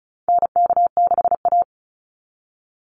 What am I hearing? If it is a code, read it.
Morse code, heard as a keyed tone, about 34 wpm, 700 Hz. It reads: NK6A
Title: NA SOTA Chaser Callsigns 35wpm